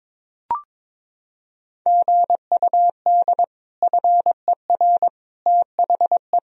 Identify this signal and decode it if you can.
Morse code, heard as a keyed tone, about 22 wpm, 700 Hz. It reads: GUD FER THE
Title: Sentences from Top 300 Words and Abbreviations 22wpm